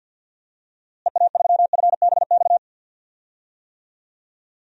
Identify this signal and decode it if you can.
Morse code, heard as a keyed tone, about 49 wpm, 700 Hz. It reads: EA3FBX
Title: DXCC-100 Call Signs - 6 Characters - Repeated 2x - 50wpm